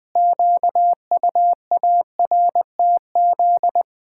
Morse code, heard as a keyed tone, about 21 wpm, 700 Hz. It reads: QUARTZ
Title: Pangram Challenge v2 - 20wpm